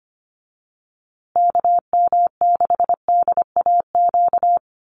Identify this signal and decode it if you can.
Morse code, heard as a keyed tone, about 24 wpm, 700 Hz. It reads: KM6DAQ
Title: USA Call Signs Repeated 3x 25wpm